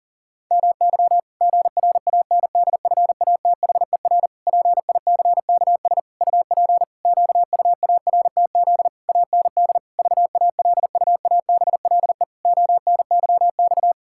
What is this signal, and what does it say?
Morse code, heard as a keyed tone, about 40 wpm, 700 Hz. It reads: MY GRANDFATHER PICKS UP QUARTZ AND VALUABLE ONYX
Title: Pangram Challenge v2 - 40wpm